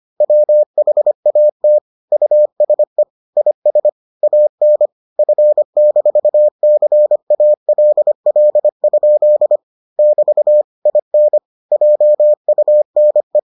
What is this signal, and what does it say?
Morse code, heard as a keyed tone, about 25 wpm, 600 Hz. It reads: WHAT USE IS AN F-CALL? <BT> IN JUNE